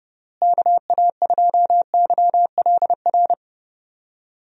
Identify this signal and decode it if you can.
Morse code, heard as a keyed tone, about 30 wpm, 700 Hz. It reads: KA2YLR
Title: USA Call Signs - 6 Characters - Repeated 3x - 30wpm